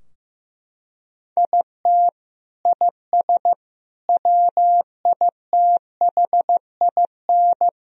Morse code, heard as a keyed tone, about 15 wpm, 700 Hz. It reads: IT IS WITHIN
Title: Sentences from Top 1000 Words - Review Part 4 - 15wpm